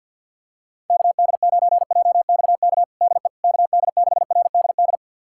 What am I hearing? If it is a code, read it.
Morse code, heard as a keyed tone, about 46 wpm, 690 Hz. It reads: KD9JXK DE KD6RDD
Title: USA Callsign DE Callsign - 6 Characters - 50wpm